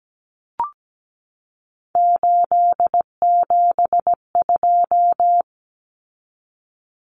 Morse code, heard as a keyed tone, about 17 wpm, 700 Hz. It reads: 872
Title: Common Cut Numbers - 3-Digit 17wpm